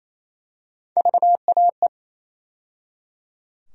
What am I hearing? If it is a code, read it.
Morse code, heard as a keyed tone, about 28 wpm, 700 Hz. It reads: VAE
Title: USA Call Sign Postfixes - 3 Letters 28wpm